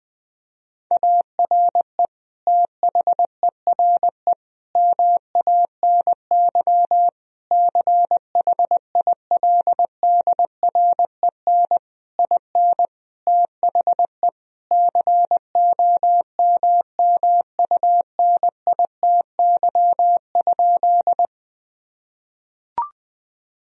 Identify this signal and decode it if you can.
Morse code, heard as a keyed tone, about 21 wpm, 700 Hz. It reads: ARE THERE MANY CHILDREN IN THE COMMUNITY?
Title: Sentences from Top 1000 Words - Review Part 4 - 20wpm